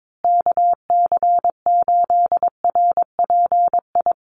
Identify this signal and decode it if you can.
Morse code, heard as a keyed tone, about 22 wpm, 700 Hz. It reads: KC8RPI
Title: USA Call Signs Repeated 2x 22wpm